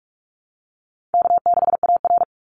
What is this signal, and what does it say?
Morse code, heard as a keyed tone, about 44 wpm, 700 Hz. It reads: K6AR
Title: Call Signs - 45wpm